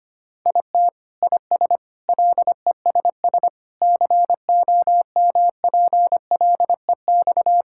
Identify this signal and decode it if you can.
Morse code, heard as a keyed tone, about 25 wpm, 700 Hz. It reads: IT IS LESS COMPLEX
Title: Sentences from Top 1000 Words - Review Part 6 - 25wpm